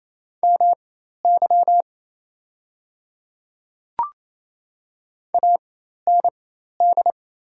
Morse code, heard as a keyed tone, about 28 wpm, spaced as for 13 wpm, 700 Hz. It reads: MY AND
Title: 4x Character Spacing - Top 100 Words - 28wpm